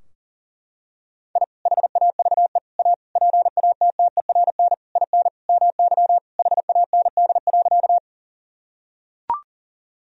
Morse code, heard as a keyed tone, about 40 wpm, 700 Hz. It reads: I HAVE A PATTERN IN MY HAND.
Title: Sentences from Top 1000 Words - Review Part 2 - 40wpm